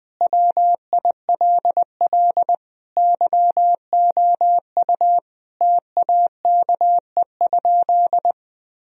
Morse code, heard as a keyed tone, about 21 wpm, 700 Hz. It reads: WILL YOU TAKE?
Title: Sentences from Top 1000 Words - Review Part 2 - 20wpm